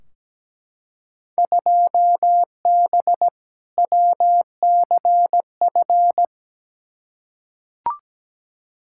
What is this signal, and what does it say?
Morse code, heard as a keyed tone, about 17 wpm, 700 Hz. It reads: 2B WCF
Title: Field Day Exchanges Encore - 17wpm